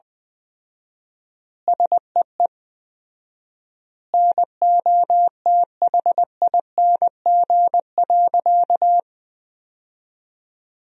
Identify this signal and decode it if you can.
Morse code, heard as a keyed tone, about 20 wpm, 700 Hz. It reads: SEE NOTHING.
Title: 4x Word-Spacing - Sentences from Top 500 Words - 20wpm